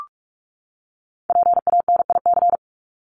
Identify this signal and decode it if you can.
Morse code, heard as a keyed tone, about 44 wpm, 700 Hz. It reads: PANIC
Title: Top 1751-2000 Words 45wpm